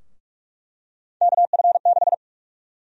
Morse code, heard as a keyed tone, about 41 wpm, 700 Hz. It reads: KF6
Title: USA Call Sign Prefixes - Repeated 2x - 45wpm